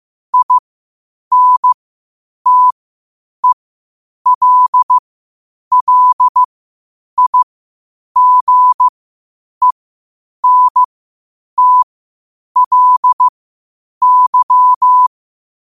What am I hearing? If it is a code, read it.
Morse code, heard as a keyed tone, about 15 wpm, spaced as for 8 wpm, 1000 Hz. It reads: INTELLIGENTLY
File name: qotdfiles / 8wpm